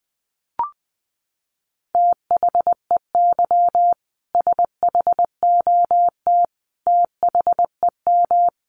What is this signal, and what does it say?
Morse code, heard as a keyed tone, about 20 wpm, 700 Hz. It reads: THEY SHOT THEM
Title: Sentences from Top 1100 Words 20wpm